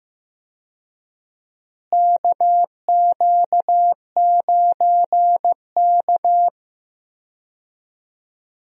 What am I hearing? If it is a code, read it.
Morse code, heard as a keyed tone, about 15 wpm, 700 Hz. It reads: KQ9K
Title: USA Call Signs - 4 Characters - Repeated 2x - 15wpm